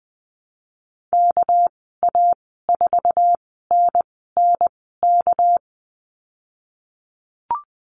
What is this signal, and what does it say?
Morse code, heard as a keyed tone, about 20 wpm, spaced as for 15 wpm, 700 Hz. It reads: KA4NNK
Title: 213 Lesson 34g - Number 8 - Call Signs 20wpm